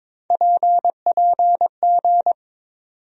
Morse code, heard as a keyed tone, about 22 wpm, 700 Hz. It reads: PPG
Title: USA Call Sign Postfixes - 3 Letters 22wpm